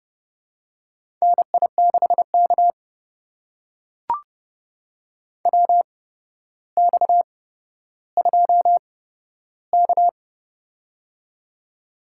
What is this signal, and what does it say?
Morse code, heard as a keyed tone, about 30 wpm, spaced as for 13 wpm, 700 Hz. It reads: NI6K WX2K
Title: USA Call Signs - 4 Characters - 8x Character Spacing - 30wpm